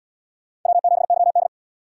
Morse code, heard as a keyed tone, about 77 wpm, 700 Hz. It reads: W7YD